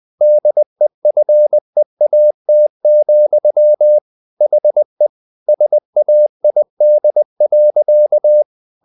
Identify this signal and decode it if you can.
Morse code, heard as a keyed tone, about 20 wpm, 600 Hz. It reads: DEFEAT, HE SAID.
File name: the old man and the sea part 2 20wpm